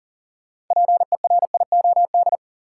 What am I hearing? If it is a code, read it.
Morse code, heard as a keyed tone, about 40 wpm, 700 Hz. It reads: PERIOD
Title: Sets of 5 Words - New Word Drill - 40wpm